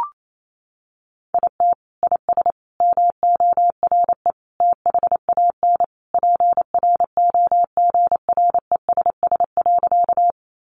Morse code, heard as a keyed tone, about 28 wpm, 700 Hz. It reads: IT IS MORE THAN PROGRESS.
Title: Sentences from Top 1400 Words 28wpm